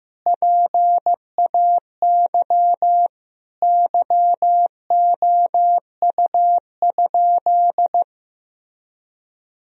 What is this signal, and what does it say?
Morse code, heard as a keyed tone, about 15 wpm, 700 Hz. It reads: PAY YOU?
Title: Sentences from Top 1000 Words - Review Part 3 - 15wpm